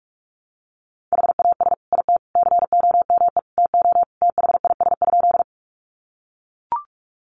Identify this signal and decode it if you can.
Morse code, heard as a keyed tone, about 44 wpm, 700 Hz. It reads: HAS IT COME TO THIS?
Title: Sentences from Top 100 Words 45wpm